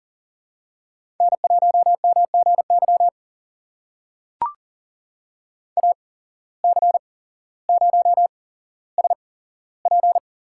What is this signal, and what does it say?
Morse code, heard as a keyed tone, about 37 wpm, spaced as for 17 wpm, 700 Hz. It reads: N1MGY AC0SP
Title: USA Call Signs - 8x Character Spacing - 40wpm